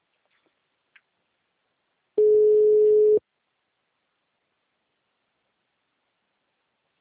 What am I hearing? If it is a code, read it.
Ringback tone (Europe, 425 Hz).
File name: out- -2001- - - .38929